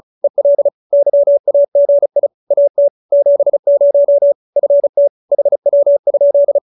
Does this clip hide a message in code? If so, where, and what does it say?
Yes: Morse code from 0.2 to 6.6 s, heard as a keyed tone, about 35 wpm, 590 Hz. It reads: EL YAGI AT 70 FT HW?